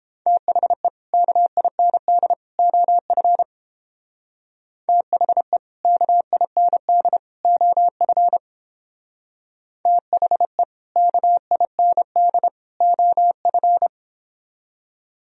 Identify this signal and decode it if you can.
Morse code, heard as a keyed tone, about 29 wpm, 700 Hz. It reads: THE KIND OF THE KIND OF THE KIND OF